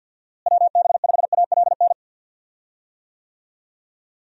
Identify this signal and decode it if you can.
Morse code, heard as a keyed tone, about 52 wpm, 700 Hz. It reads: WB5ALN
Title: USA Call Signs - 6 Characters - Repeated 2x - 50wpm